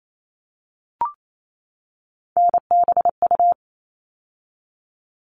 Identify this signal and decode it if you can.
Morse code, heard as a keyed tone, about 28 wpm, 700 Hz. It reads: NBU
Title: All Cut Numbers - 3-Digit 28wpm